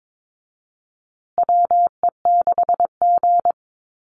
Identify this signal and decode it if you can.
Morse code, heard as a keyed tone, about 22 wpm, 700 Hz. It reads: WE6G
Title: USA Call Signs Repeated 3x 22wpm